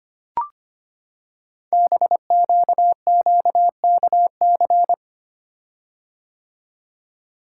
Morse code, heard as a keyed tone, about 25 wpm, 700 Hz. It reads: BQQKC
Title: Code Groups - 5 Letters - 25wpm